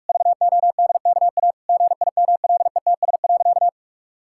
Morse code, heard as a keyed tone, about 46 wpm, 700 Hz. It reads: VODKA GIMLETS.
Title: Pangram Challenge v2 - 45wpm